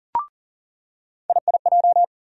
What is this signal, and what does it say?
Morse code, heard as a keyed tone, about 40 wpm, 700 Hz. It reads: IIJ
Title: USA Call Sign Postfixes - 3 Letters 40wpm